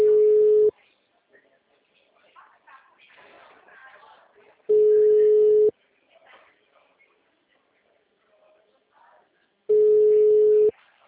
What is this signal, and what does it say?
Ringback tone (Europe, 425 Hz), with speech.
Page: Limeira